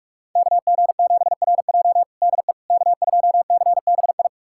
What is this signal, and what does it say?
Morse code, heard as a keyed tone, about 46 wpm, 700 Hz. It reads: KG7RJ DE K2CBI
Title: USA Callsign DE Callsign - 5 Characters - 45wpm